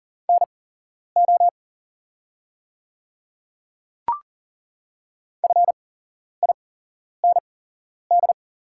Morse code, heard as a keyed tone, about 40 wpm, spaced as for 11 wpm, 700 Hz. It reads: NO FIND
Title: 8x Character Spacing - Top 100 Words - 40wpm